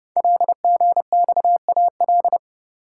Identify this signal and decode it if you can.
Morse code, heard as a keyed tone, about 30 wpm, 690 Hz. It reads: LGXAL
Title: Code Groups - 5 Letters - 30wpm